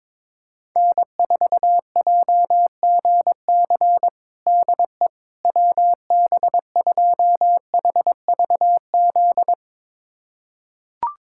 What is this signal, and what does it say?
Morse code, heard as a keyed tone, about 22 wpm, 700 Hz. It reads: N4JGC DE WB2HVZ
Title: USA Callsign DE Callsign 22wpm